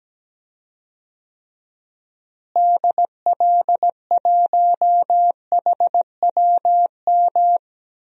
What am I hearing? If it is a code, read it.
Morse code, heard as a keyed tone, about 17 wpm, 700 Hz. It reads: DL1HWM
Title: DXCC-100 Call Signs - 6 Characters - Repeated 3x - 17wpm